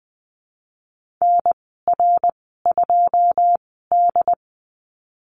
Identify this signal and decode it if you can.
Morse code, heard as a keyed tone, about 20 wpm, spaced as for 15 wpm, 700 Hz. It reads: NR2D
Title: 193 Lesson 31g - Letter v - Call Signs 20wpm